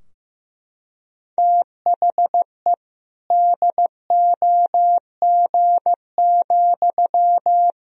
Morse code, heard as a keyed tone, about 15 wpm, 700 Hz. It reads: THE DOG,
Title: Sentences from Top 1000 Words - Review Part 3 - 15wpm